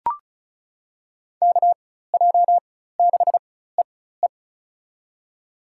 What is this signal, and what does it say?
Morse code, heard as a keyed tone, about 36 wpm, spaced as for 16 wpm, 700 Hz. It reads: KJ6EE
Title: USA Call Signs - 4x Character Spacing - 35wpm